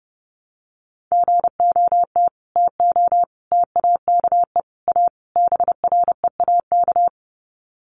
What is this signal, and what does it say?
Morse code, heard as a keyed tone, about 28 wpm, 700 Hz. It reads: GOT TO TAKE A BREAK
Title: Sets of 5 Words Encore 30wpm